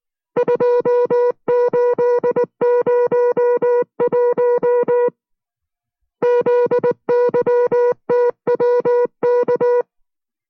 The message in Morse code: 2801 ZYTWK